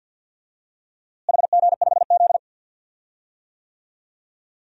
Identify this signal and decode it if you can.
Morse code, heard as a keyed tone, about 51 wpm, 700 Hz. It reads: HG5Z
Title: DXCC-100 Call Signs - 4 Characters - Repeated 2x - 50wpm